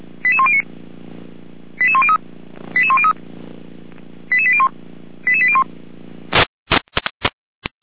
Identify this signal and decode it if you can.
Five-tone selective call (ZVEI), heard as quick runs of five tones, tones of about 80 ms. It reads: 88109 88103 88103 88801 88801
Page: audio.f85360.feuerwehr-orlamuende.org